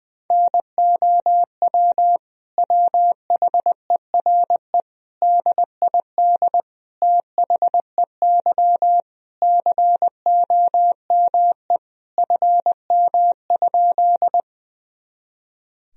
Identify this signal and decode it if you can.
Morse code, heard as a keyed tone, about 20 wpm, 700 Hz. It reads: NOW WHERE DID THEY COME FM?
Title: Sentences from Top 200 Words and Abbreviations 20wpm